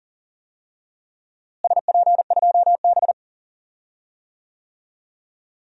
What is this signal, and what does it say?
Morse code, heard as a keyed tone, about 38 wpm, 700 Hz. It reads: SP2B
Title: DXCC-100 Call Signs - 4 Characters - Repeated 2x - 40wpm